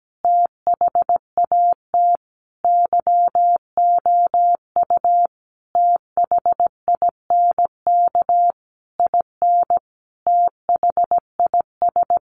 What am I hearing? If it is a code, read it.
Morse code, heard as a keyed tone, about 17 wpm, 700 Hz. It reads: THAT YOU THINK IN THIS
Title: Sentences from Top 1200 Words 17wpm